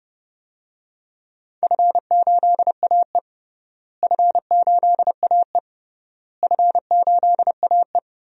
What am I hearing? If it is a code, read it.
Morse code, heard as a keyed tone, about 32 wpm, 700 Hz. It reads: F8AE F8AE F8AE
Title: DXCC-100 Call Signs - 4 Characters - Repeated 3x - 30wpm